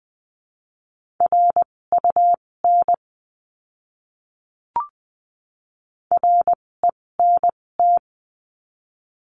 Morse code, heard as a keyed tone, about 20 wpm, spaced as for 16 wpm, 700 Hz. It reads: RUN RENT
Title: 256 Lesson 42a - Words 12-20wpm